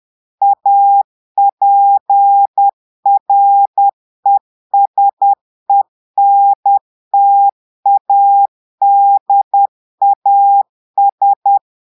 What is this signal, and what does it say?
Morse code, heard as a keyed tone, about 10 wpm, 800 Hz. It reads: APRESENTADAS
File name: portugais texte en clair 2 vitesse 10 mots minutes